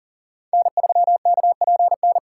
Morse code, heard as a keyed tone, about 40 wpm, 700 Hz. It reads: N3KPN
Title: USA Call Signs - 6x Character Spacing - 40wpm